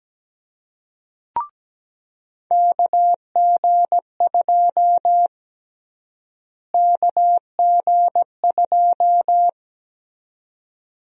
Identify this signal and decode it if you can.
Morse code, heard as a keyed tone, about 17 wpm, 700 Hz. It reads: KG2 KG2
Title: USA Call Sign Prefixes - Repeated 2x - 17wpm